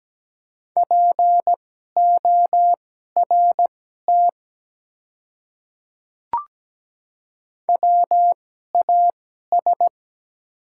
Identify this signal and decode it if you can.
Morse code, heard as a keyed tone, about 17 wpm, spaced as for 12 wpm, 700 Hz. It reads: PORT WAS
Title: 2x Character Spacing - Top 200 Words - 17wpm